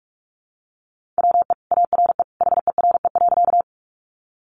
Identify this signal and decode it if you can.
Morse code, heard as a keyed tone, about 44 wpm, 700 Hz. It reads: WE ARE HERE.
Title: Sentences from Top 200 Words 45wpm